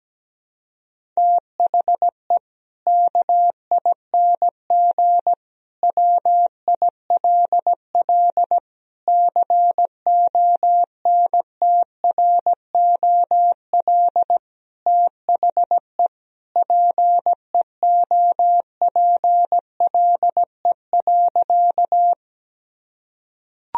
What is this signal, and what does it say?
Morse code, heard as a keyed tone, about 17 wpm, 700 Hz. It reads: THE KING WILL CONTROL THE PEOPLE.
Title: Sentences from Top 1000 Words - Review Part 3 - 17wpm